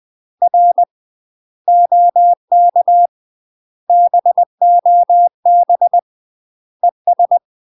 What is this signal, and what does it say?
Morse code, heard as a keyed tone, about 20 wpm, 700 Hz. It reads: R OK BOB ES
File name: UK MASTER QSOs TRAINING / 20WPM / UK MASTER QSOs 0065